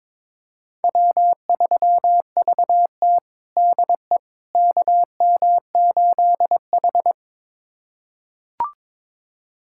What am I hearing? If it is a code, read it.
Morse code, heard as a keyed tone, about 22 wpm, 700 Hz. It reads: W3VT DE KM8H
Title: USA Callsign DE Callsign - 4 Characters - 22wpm